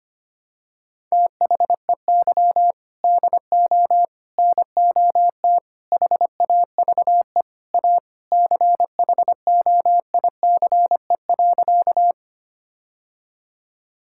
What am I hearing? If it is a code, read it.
Morse code, heard as a keyed tone, about 25 wpm, 700 Hz. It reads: THEY DO NOT HAVE A CHOICE.
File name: practice / 25wpm / Sentences with 4-Word Phrases - 25wpm